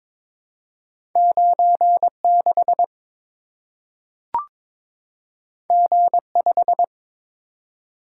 Morse code, heard as a keyed tone, about 22 wpm, 700 Hz. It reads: 96 G5
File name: practice / 22wpm / Two Letter-Number Combinations - 22wpm